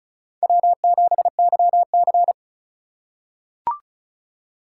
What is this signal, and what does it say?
Morse code, heard as a keyed tone, about 36 wpm, 700 Hz. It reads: W7YC
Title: USA Call Signs - 2x Character Spacing - 35wpm